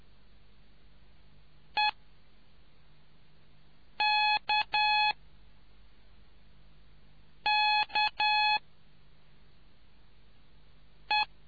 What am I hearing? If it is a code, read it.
Morse code, heard as a keyed tone, about 10 wpm, spaced as for 3 wpm, 850 Hz. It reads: EKKE